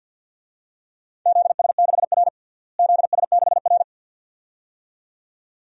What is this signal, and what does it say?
Morse code, heard as a keyed tone, about 53 wpm, 700 Hz. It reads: ZS6R ZS6R
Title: DXCC-100 Call Signs - 4 Characters - Repeated 2x - 50wpm